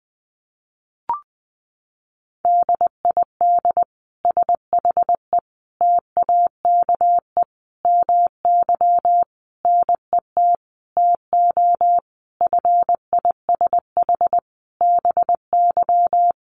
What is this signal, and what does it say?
Morse code, heard as a keyed tone, about 20 wpm, 700 Hz. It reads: DID SHE TAKE MY NET TO FISH BY
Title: Sentences from Top 1400 Words 20wpm